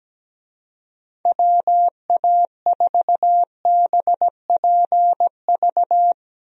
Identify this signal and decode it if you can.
Morse code, heard as a keyed tone, about 17 wpm, 700 Hz. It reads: WA4BPV
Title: USA Call Signs - 6 Characters - Repeated 3x - 17wpm